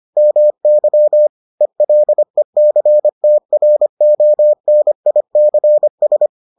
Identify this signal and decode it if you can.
Morse code, heard as a keyed tone, about 25 wpm, 600 Hz. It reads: MY ELECTRONICS